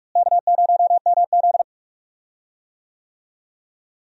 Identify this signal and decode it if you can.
Morse code, heard as a keyed tone, about 46 wpm, 700 Hz. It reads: K0MZ
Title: USA Call Signs - 4 Characters - Repeated 3x - 45wpm